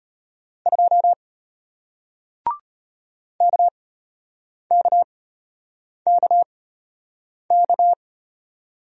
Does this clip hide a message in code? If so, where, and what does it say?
Yes: Morse code from 0.7 to 7.9 s, heard as a keyed tone, about 34 wpm, spaced as for 8 wpm, 700 Hz. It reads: 2 KKKK